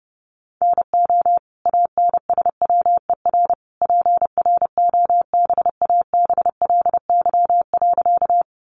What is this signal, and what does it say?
Morse code, heard as a keyed tone, about 30 wpm, 700 Hz. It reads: NO ANSWER PROBABLY.
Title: Sentences from Top 700 Words 30wpm